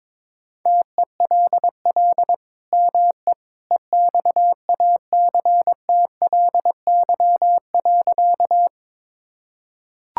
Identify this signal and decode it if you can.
Morse code, heard as a keyed tone, about 21 wpm, 700 Hz. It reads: TELL ME EXACTLY.
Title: Sentences from Top 1000 Words - Review Part 6 - 22wpm